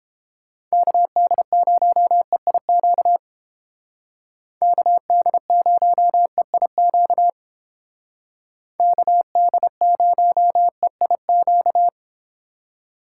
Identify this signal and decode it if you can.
Morse code, heard as a keyed tone, about 29 wpm, 700 Hz. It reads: KD0EIQ KD0EIQ KD0EIQ